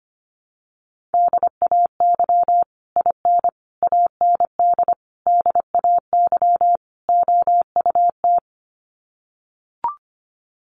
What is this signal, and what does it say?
Morse code, heard as a keyed tone, about 25 wpm, 700 Hz. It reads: DAY IN AND DAY OUT
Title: Sets of 5 Words Encore2 25wpm